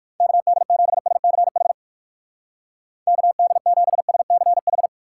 Morse code, heard as a keyed tone, about 56 wpm, 700 Hz. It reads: KD7SCH KD7SCH
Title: Speed Racing - USA 2x3 Call Signs 45wpm